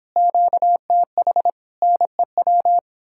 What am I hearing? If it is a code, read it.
Morse code, heard as a keyed tone, about 29 wpm, 700 Hz. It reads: QTH NEW